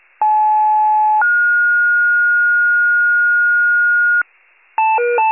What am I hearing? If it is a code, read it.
Two-tone page, heard as a one-second tone, then a three-second tone, about 860 then 1430 Hz, with noise. It is heard from 0.2 to 4.2 s.
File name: pager / 75 FIRE / wcalert